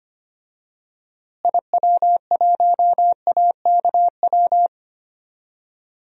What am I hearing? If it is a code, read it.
Morse code, heard as a keyed tone, about 25 wpm, 700 Hz. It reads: IW1AKW
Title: DXCC-100 Call Signs - 6 Characters - Repeated 3x - 25wpm